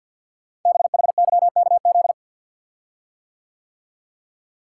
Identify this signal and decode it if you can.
Morse code, heard as a keyed tone, about 49 wpm, 700 Hz. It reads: BHYKZ
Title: Code Groups - 5 Letters - 50wpm